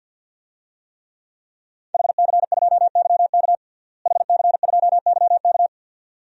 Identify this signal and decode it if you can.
Morse code, heard as a keyed tone, about 52 wpm, 700 Hz. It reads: HC2YK HC2YK
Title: DXCC-100 Call Signs - 5 Characters - Repeated 2x - 50wpm